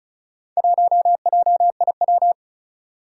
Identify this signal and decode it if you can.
Morse code, heard as a keyed tone, about 36 wpm, 700 Hz. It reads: 1JIW
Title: Four Challenging Letter-Number Combinations 35wpm